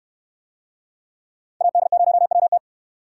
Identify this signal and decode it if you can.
Morse code, heard as a keyed tone, about 68 wpm, 700 Hz. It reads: AD0WT